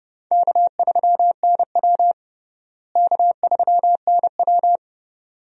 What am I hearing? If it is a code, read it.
Morse code, heard as a keyed tone, about 30 wpm, 700 Hz. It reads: K3NW K3NW
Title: USA Call Signs - 4 Characters - Repeated 2x - 30wpm